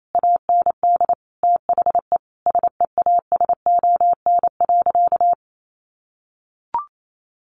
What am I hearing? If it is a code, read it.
Morse code, heard as a keyed tone, about 28 wpm, 700 Hz. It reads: AND THE SEASON.
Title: Sentences from Top 1500 Words 28wpm